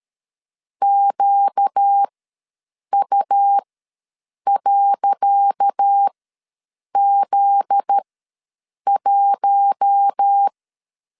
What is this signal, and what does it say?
Morse code, heard as a keyed tone, about 13 wpm, spaced as for 7 wpm, 800 Hz. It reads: QU.Z1
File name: LEZIONI / 10 SIGNS / 10C DOT ws 12wpm 3548